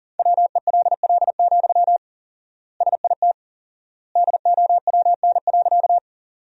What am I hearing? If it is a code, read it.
Morse code, heard as a keyed tone, about 40 wpm, 700 Hz. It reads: WELL, SIT DOWN.
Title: 4x Word-Spacing - Sentences from Top 500 Words - 40wpm